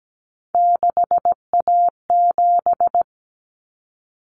Morse code, heard as a keyed tone, about 17 wpm, 700 Hz. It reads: 6A7
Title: Common Cut Numbers - 3-Digit 17wpm